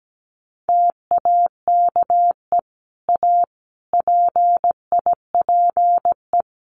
Morse code, heard as a keyed tone, about 17 wpm, 700 Hz. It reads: TAKE A PIPE
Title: Sentences from Top 1500 Words 17wpm